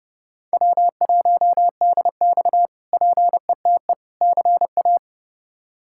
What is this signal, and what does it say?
Morse code, heard as a keyed tone, about 30 wpm, 700 Hz. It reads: W1DX PETE CA
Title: K1USN SST QSO Practice - 30wpm